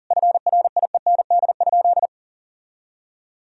Morse code, heard as a keyed tone, about 40 wpm, 700 Hz. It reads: FRIEND?
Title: Sentences from Top 1000 Words - Review Part 4 - 40wpm